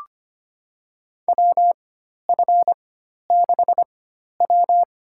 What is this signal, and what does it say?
Morse code, heard as a keyed tone, about 25 wpm, spaced as for 12 wpm, 700 Hz. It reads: WF6W
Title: USA Call Signs - 4 Characters - 4x Character Spacing - 25wpm